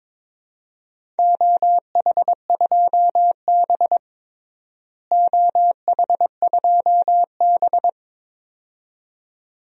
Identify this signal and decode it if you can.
Morse code, heard as a keyed tone, about 22 wpm, 700 Hz. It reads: OH2B OH2B
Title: DXCC-100 Call Signs - 4 Characters - Repeated 2x - 22wpm